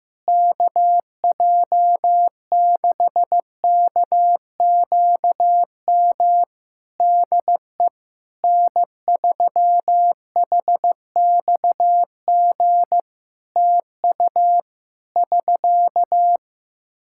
Morse code, heard as a keyed tone, about 15 wpm, 700 Hz. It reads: KJ6KQM DE N3HXG TU <SK>
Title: Practice QSO - Exchange 4 - Ending - 15wpm